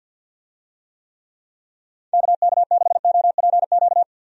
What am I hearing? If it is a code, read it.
Morse code, heard as a keyed tone, about 51 wpm, 700 Hz. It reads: KK6OPQ
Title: USA Call Signs - 6 Characters - 50wpm